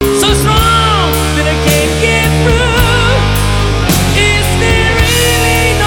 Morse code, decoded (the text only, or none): none